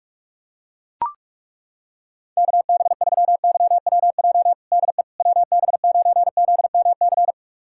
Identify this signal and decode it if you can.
Morse code, heard as a keyed tone, about 41 wpm, 690 Hz. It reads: KB3YWJ DE WB9ZMC
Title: USA Callsign DE Callsign 45wpm